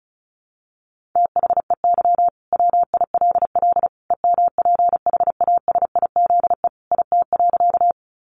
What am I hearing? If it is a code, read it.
Morse code, heard as a keyed tone, about 34 wpm, 700 Hz. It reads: THEY WILL EMPHASIZE IT.
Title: Sentences from Top 1500 Words 35wpm